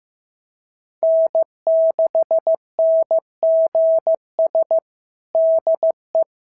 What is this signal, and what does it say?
Morse code, heard as a keyed tone, about 14 wpm, 650 Hz. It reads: N6NGS DE
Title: SOTA QSO Practice 15wpm